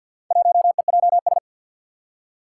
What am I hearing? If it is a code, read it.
Morse code, heard as a keyed tone, about 51 wpm, 700 Hz. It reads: 1EJS
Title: Four Challenging Letter-Number Combinations 50wpm